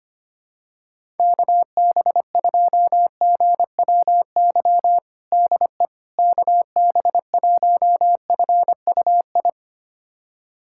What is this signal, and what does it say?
Morse code, heard as a keyed tone, about 25 wpm, 700 Hz. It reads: KB2GWY DE KB1FUI
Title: USA Callsign DE Callsign - 6 Characters - 25wpm